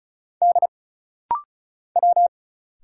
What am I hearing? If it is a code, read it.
Morse code, heard as a keyed tone, about 36 wpm, 700 Hz. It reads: D W
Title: Single Letter-Number Alt - Rapid-Fire 35wpm